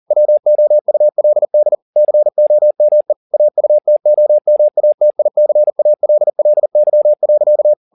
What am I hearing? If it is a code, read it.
Morse code, heard as a keyed tone, about 40 wpm, 600 Hz. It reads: WOULD COME AUTOMATICALLY.